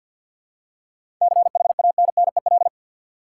Morse code, heard as a keyed tone, about 52 wpm, 700 Hz. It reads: CHANNEL
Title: Seven Letter Words - 50wpm